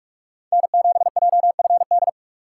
Morse code, heard as a keyed tone, about 45 wpm, 700 Hz. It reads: N7JFD
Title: USA Call Signs - 3x Character Spacing - 45wpm